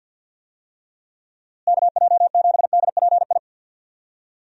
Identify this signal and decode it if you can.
Morse code, heard as a keyed tone, about 50 wpm, 700 Hz. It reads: KJ7DPI
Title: USA Call Signs - 6 Characters - Repeated 3x - 50wpm